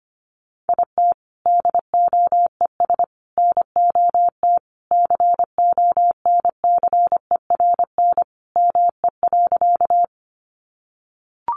Morse code, heard as a keyed tone, about 25 wpm, 700 Hz. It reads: IT DOES NOT CONCERN ME.
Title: Sentences from Top 1000 Words 25wpm